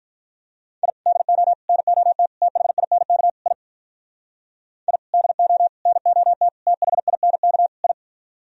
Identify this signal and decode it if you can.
Morse code, heard as a keyed tone, about 43 wpm, 700 Hz. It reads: I DO NOT THINK I I DO NOT THINK I
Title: Speed-Racing - Sets of 5 Words 35wpm